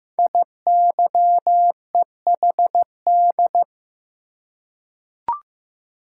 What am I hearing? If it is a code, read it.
Morse code, heard as a keyed tone, about 15 wpm, 700 Hz. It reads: IYEHD
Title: Code Groups - 5 Letters - 15wpm